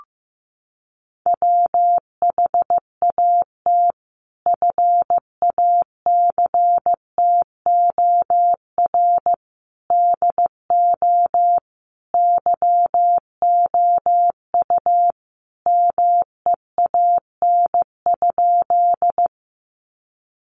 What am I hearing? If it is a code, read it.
Morse code, heard as a keyed tone, about 15 wpm, 700 Hz. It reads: WHAT FACTOR DO YOU MEAN?
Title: Sentences from Top 1300 Words 15wpm